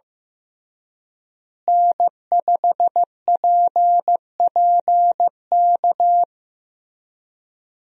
Morse code, heard as a keyed tone, about 15 wpm, 700 Hz. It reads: N5PPK
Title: USA Call Signs - 5 Characters - Repeated 3x - 15wpm